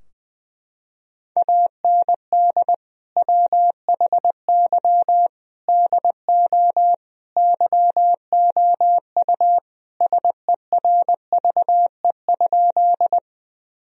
Morse code, heard as a keyed tone, about 21 wpm, 700 Hz. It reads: AND WHY DO YOU SERVE?
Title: Sentences from Top 1000 Words - Review Part 2 - 20wpm